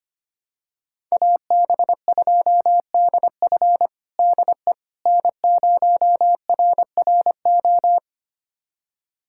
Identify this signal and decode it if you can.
Morse code, heard as a keyed tone, about 25 wpm, 700 Hz. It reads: AB2DF DE N0RRO
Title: USA Callsign DE Callsign 25wpm